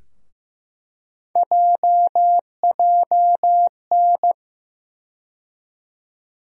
Morse code, heard as a keyed tone, about 15 wpm, 700 Hz. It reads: JJN
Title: USA Call Sign Postfixes - 3 Letters 15wpm